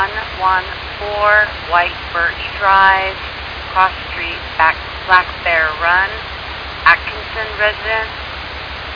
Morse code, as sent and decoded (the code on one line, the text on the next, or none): none